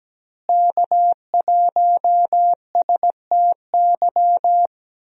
Morse code, heard as a keyed tone, about 17 wpm, 700 Hz. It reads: K1STY
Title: USA Call Signs - 5 Characters - Repeated 3x - 17wpm